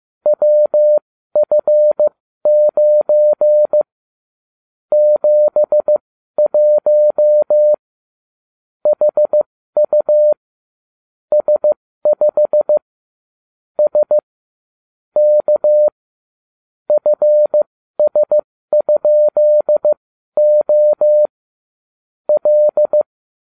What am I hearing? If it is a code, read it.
Morse code, heard as a keyed tone, about 15 wpm, spaced as for 12 wpm, 600 Hz. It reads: WF9 71 HU S5 S K FS?O L